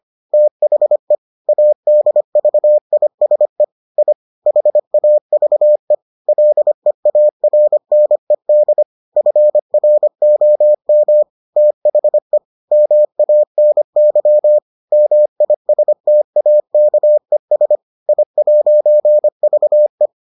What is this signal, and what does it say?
Morse code, heard as a keyed tone, about 25 wpm, 600 Hz. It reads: THE ADVISE I HAVE LEARNED FROM THE MANY MISTAKES I'VE